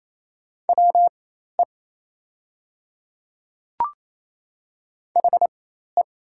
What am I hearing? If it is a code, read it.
Morse code, heard as a keyed tone, about 28 wpm, spaced as for 13 wpm, 690 Hz. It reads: WE HE
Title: 4x Character Spacing - Top 100 Words - 28wpm